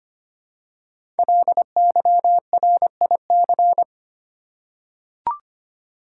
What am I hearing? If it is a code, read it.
Morse code, heard as a keyed tone, about 25 wpm, 700 Hz. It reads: LYRIC
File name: 5 Letter Words - First 3 Letters Given - 25wpm